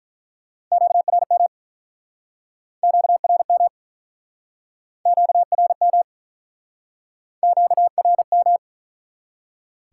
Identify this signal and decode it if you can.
Morse code, heard as a keyed tone, about 50 wpm, 700 Hz. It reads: QRM QRM QRM QRM